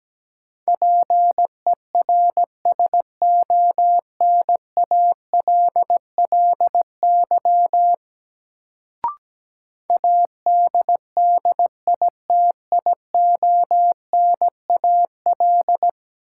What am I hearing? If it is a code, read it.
Morse code, heard as a keyed tone, about 17 wpm, 700 Hz. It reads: PERSONALLY ADDITIONAL
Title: Sentences from Top 1400 Words - New Word Drill Alt - 17wpm